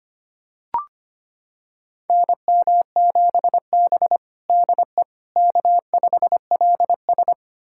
Morse code, heard as a keyed tone, about 25 wpm, 700 Hz. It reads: NM7B DE K5LS
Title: USA Callsign DE Callsign - 4 Characters - 25wpm